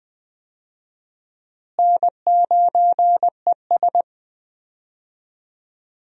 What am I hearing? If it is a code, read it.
Morse code, heard as a keyed tone, about 20 wpm, 700 Hz. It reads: N9ES
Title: USA Call Signs - 4 Characters - Repeated 3x - 20wpm